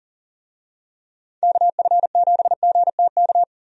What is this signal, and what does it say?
Morse code, heard as a keyed tone, about 40 wpm, 700 Hz. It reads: KF7GTK
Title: USA Call Signs - 6 Characters - Repeated 2x - 40wpm